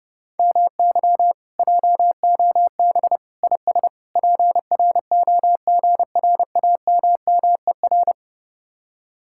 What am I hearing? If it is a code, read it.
Morse code, heard as a keyed tone, about 31 wpm, 700 Hz. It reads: MY JOB IS PROGRAMMER